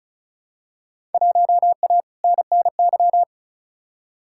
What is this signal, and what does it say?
Morse code, heard as a keyed tone, about 36 wpm, 700 Hz. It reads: 1A NNY
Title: Field Day Exchanges Encore - 35wpm